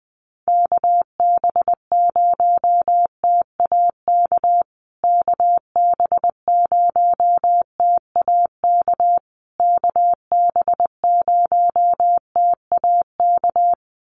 Morse code, heard as a keyed tone, about 20 wpm, 700 Hz. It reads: KB0TAK KB0TAK KB0TAK
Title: USA Call Signs Repeated 3x 20wpm